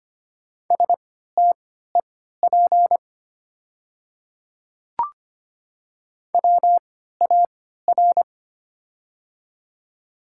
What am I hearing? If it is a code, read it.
Morse code, heard as a keyed tone, about 25 wpm, spaced as for 14 wpm, 700 Hz. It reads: STEP WAR
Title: 3x Character Spacing - Top 500 Words - 25wpm